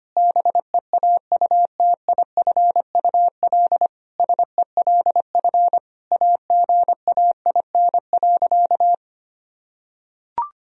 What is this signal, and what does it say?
Morse code, heard as a keyed tone, about 25 wpm, 700 Hz. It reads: BEAUTIFUL SELF AGAIN.
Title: Sentences from Top 1000 Words - Review Part 1 - 25wpm